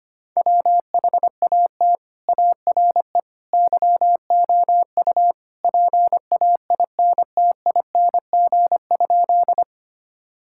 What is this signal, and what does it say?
Morse code, heard as a keyed tone, about 25 wpm, 700 Hz. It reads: WHAT ARE YOU PAINTING?
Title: Sentences from Top 1000 Words - Review Part 6 - 25wpm